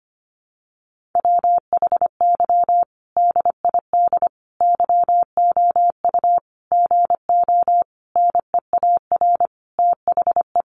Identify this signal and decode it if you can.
Morse code, heard as a keyed tone, about 25 wpm, 700 Hz. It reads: WHY DID YOU GO NEAR THE
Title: Sentences from Top 1100 Words 25wpm